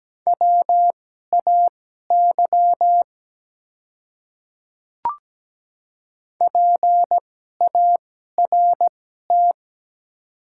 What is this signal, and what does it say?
Morse code, heard as a keyed tone, about 17 wpm, spaced as for 12 wpm, 700 Hz. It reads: WAY PART
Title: 2x Character Spacing - Top 200 Words - 17wpm